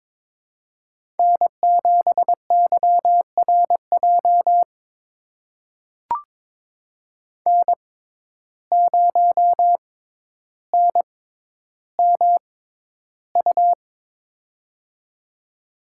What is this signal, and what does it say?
Morse code, heard as a keyed tone, about 23 wpm, spaced as for 11 wpm, 700 Hz. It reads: N7YRJ N0NMU
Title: USA Call Signs - 5 Characters - 6x Character Spacing - 22wpm